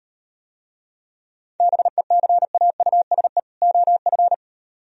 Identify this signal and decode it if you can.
Morse code, heard as a keyed tone, about 38 wpm, 700 Hz. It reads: BECAUSE OF